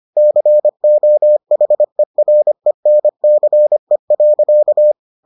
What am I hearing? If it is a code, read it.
Morse code, heard as a keyed tone, about 25 wpm, 600 Hz. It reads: COHERENCE.